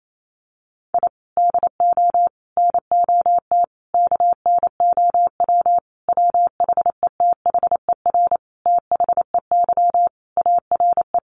Morse code, heard as a keyed tone, about 28 wpm, 700 Hz. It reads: I DO NOT KNOW WHETHER THEY ARE
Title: Sentences from Top 1500 Words 28wpm